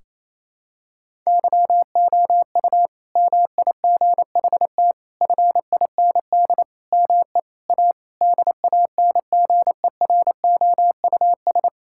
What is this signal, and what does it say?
Morse code, heard as a keyed tone, about 27 wpm, 700 Hz. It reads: YOU MIGHT FIND ME A DANGEROUS
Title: Sentences from Top 1600 Words 28wpm